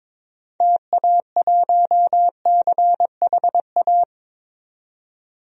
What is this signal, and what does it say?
Morse code, heard as a keyed tone, about 21 wpm, 700 Hz. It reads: TA1CHA
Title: DXCC-100 Call Signs - 6 Characters - 22wpm